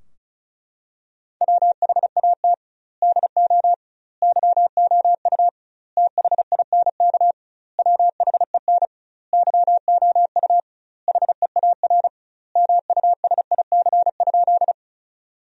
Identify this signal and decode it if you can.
Morse code, heard as a keyed tone, about 36 wpm, 700 Hz. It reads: WHAT DO YOU THINK WHEN YOU HEAR MUSIC?
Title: 2x Word-Spacing - Sentences from Top 300 Words - 35wpm